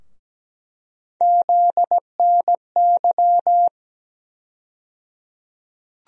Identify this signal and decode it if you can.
Morse code, heard as a keyed tone, about 17 wpm, 700 Hz. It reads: ZNY
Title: USA Call Sign Postfixes - 3 Letters 17wpm